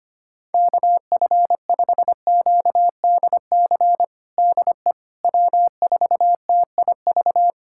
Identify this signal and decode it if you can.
Morse code, heard as a keyed tone, about 25 wpm, 700 Hz. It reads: KF5QDC DE W4TIV
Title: USA Callsign DE Callsign 25wpm